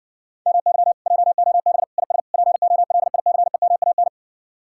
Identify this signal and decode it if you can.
Morse code, heard as a keyed tone, about 69 wpm, 700 Hz. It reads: MY JOB IS POLICEMAN